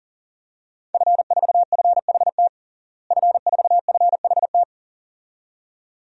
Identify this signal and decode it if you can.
Morse code, heard as a keyed tone, about 40 wpm, 700 Hz. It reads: F4FHT F4FHT
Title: DXCC-100 Call Signs - 5 Characters - Repeated 2x - 40wpm